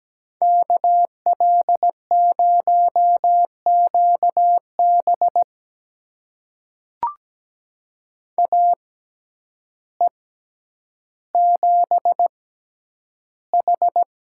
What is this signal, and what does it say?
Morse code, heard as a keyed tone, about 17 wpm, spaced as for 9 wpm, 700 Hz. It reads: KL0QB AE7H
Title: USA Call Signs - 5 Characters - 6x Character Spacing - 17wpm